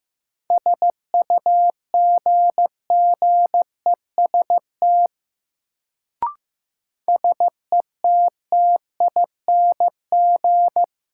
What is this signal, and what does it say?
Morse code, heard as a keyed tone, about 15 wpm, 700 Hz. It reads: SUGGEST SETTING
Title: Sentences from Top 1300 Words - New Word Drill Alt - 15wpm